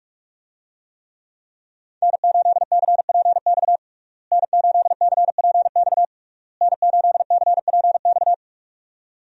Morse code, heard as a keyed tone, about 46 wpm, 700 Hz. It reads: N8CPX N8CPX N8CPX
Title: USA Call Signs - 5 Characters - Repeated 3x - 45wpm